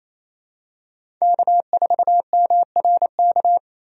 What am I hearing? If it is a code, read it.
Morse code, heard as a keyed tone, about 28 wpm, 700 Hz. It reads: K4MRK
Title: USA Call Signs - 5 Characters - 28wpm